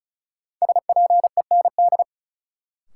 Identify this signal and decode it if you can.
Morse code, heard as a keyed tone, about 36 wpm, 700 Hz. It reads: SPEND
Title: Sets of 5 Words - New Word Drill - 35wpm